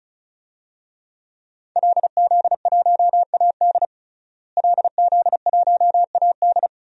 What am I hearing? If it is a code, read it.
Morse code, heard as a keyed tone, about 36 wpm, 700 Hz. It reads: LZ1AD LZ1AD
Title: DXCC-100 Call Signs - 5 Characters - Repeated 2x - 35wpm